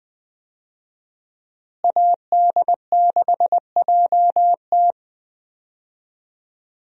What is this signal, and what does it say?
Morse code, heard as a keyed tone, about 20 wpm, 700 Hz. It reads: AD6JT
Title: USA Call Signs - 5 Characters - Repeated 2x - 20wpm